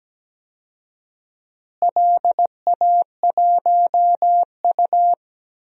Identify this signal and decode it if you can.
Morse code, heard as a keyed tone, about 17 wpm, 700 Hz. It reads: LA1U
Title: DXCC-100 Call Signs - 4 Characters - Repeated 3x - 17wpm